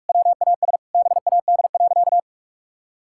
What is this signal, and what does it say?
Morse code, heard as a keyed tone, about 41 wpm, 700 Hz. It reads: WAS BAD.
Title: Sentences from Top 1000 Words - Review Part 5 - 45wpm